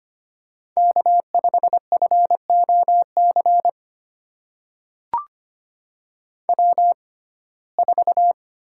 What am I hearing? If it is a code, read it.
Morse code, heard as a keyed tone, about 25 wpm, spaced as for 18 wpm, 700 Hz. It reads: K5FOC W4
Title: USA Call Signs - 5 Characters - 6x Character Spacing - 25wpm